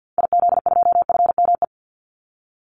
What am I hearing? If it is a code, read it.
Morse code, heard as a keyed tone, about 50 wpm, 700 Hz. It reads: IZ2FME
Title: Call Signs Repeated 3x - 50wpm